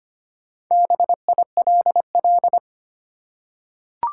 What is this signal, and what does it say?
Morse code, heard as a keyed tone, about 25 wpm, 700 Hz. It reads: BILL
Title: Four Letter Words - 25wpm